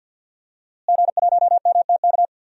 Morse code, heard as a keyed tone, about 51 wpm, 700 Hz. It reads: G1MTK
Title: DXCC-100 Call Signs - 5 Characters - Repeated 2x - 50wpm